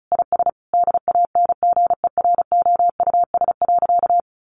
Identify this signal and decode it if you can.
Morse code, heard as a keyed tone, about 34 wpm, 700 Hz. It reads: IS DANGEROUS.
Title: Sentences from Top 1200 Words 35wpm